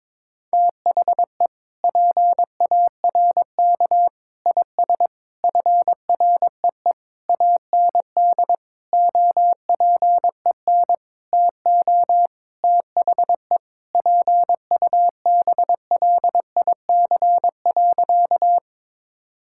Morse code, heard as a keyed tone, about 22 wpm, 700 Hz. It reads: THE PARK IS FREE AND OPEN TO THE PUBLIC.
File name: Sentences with 7-Word Phrases - 22wpm